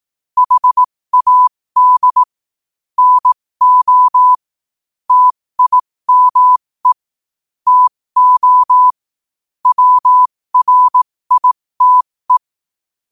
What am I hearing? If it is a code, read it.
Morse code, heard as a keyed tone, about 18 wpm, spaced as for 15 wpm, 1000 Hz. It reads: HAD NO TIME TO WRITE
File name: qotdfiles / 15wpm